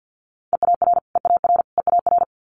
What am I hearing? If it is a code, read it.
Morse code, heard as a keyed tone, about 50 wpm, 700 Hz. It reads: EAR EAR EAR
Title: Multi-letter Phonemes Repeated 3x 50wpm